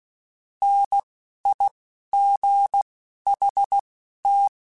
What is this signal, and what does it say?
Morse code, heard as a keyed tone, about 16 wpm, spaced as for 11 wpm, 800 Hz. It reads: NIGHT